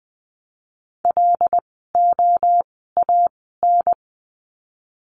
Morse code, heard as a keyed tone, about 20 wpm, spaced as for 14 wpm, 700 Hz. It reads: LOAN
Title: 075 Lesson 13f - Letter L - Words 20wpm